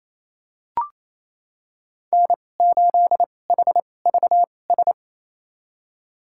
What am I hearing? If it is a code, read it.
Morse code, heard as a keyed tone, about 28 wpm, spaced as for 20 wpm, 700 Hz. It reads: N8HVS
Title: USA Call Signs - 5 Characters - 2x Character Spacing - 28wpm